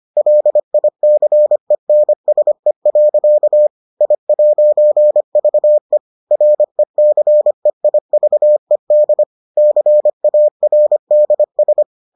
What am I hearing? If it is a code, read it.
Morse code, heard as a keyed tone, about 25 wpm, 600 Hz. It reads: LICENSE. I'VE RECEIVED CARDS